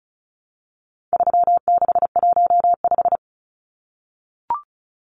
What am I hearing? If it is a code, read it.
Morse code, heard as a keyed tone, about 34 wpm, 700 Hz. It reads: 3615
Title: Numbers - 4-Digit 35wpm